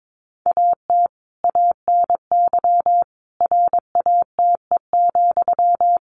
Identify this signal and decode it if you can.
Morse code, heard as a keyed tone, about 22 wpm, 700 Hz. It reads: AT ANY RATE,
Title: Sentences from Top 900 Words 22wpm